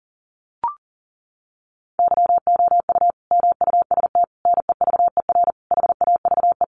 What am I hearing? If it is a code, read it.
Morse code, heard as a keyed tone, about 40 wpm, 700 Hz. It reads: YOU MUST NEVER HAVE
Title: Sentences from Top 1000 Words 40wpm